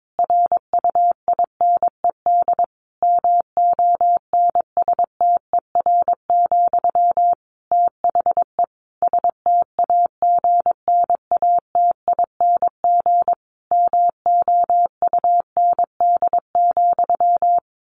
Morse code, heard as a keyed tone, about 22 wpm, 700 Hz. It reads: RUINED MONSTER, THE STAGNATING MOUND,
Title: A Sound Of Thunder 22wpm